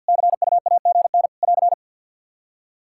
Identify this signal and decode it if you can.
Morse code, heard as a keyed tone, about 51 wpm, 700 Hz. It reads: CUAGN <AR>
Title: Practice QSO - Exchange 4 - Ending - 50wpm